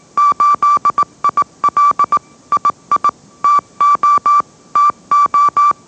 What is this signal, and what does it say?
Morse code, heard as a keyed tone, about 18 wpm, 1160 Hz. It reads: 8IL II TO TO